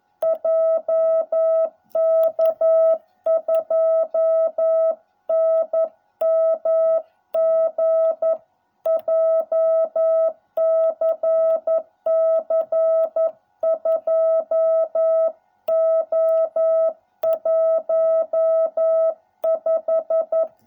Morse code, heard as a keyed tone, about 11 wpm, 660 Hz. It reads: JK2NMG JCC 2O15